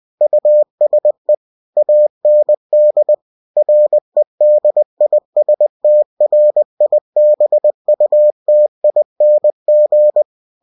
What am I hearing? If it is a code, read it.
Morse code, heard as a keyed tone, about 20 wpm, 600 Hz. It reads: USE AND REDISTRIBUTING
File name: cw-books / winnie-the-pooh / winnie-20wpm-page-0113